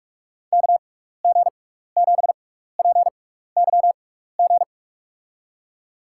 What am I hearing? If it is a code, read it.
Morse code, heard as a keyed tone, about 44 wpm, spaced as for 15 wpm, 700 Hz. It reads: KG7PYG
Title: USA Call Signs - 6 Characters - 6x Character Spacing - 45wpm